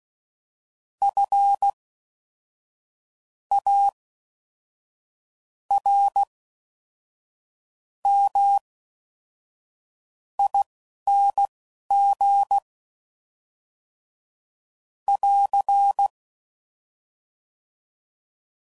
Morse code, heard as a keyed tone, about 16 wpm, spaced as for 5 wpm, 800 Hz. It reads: FARMING <AR>